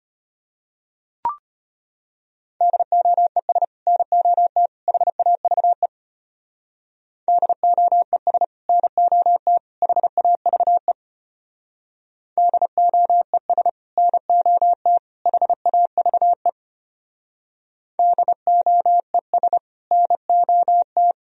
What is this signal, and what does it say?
Morse code, heard as a keyed tone, about 30 wpm, 700 Hz. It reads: DOES NOT HAVE DOES NOT HAVE DOES NOT HAVE DOES NOT
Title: Speed-Racing - Sets of 3 Words 25wpm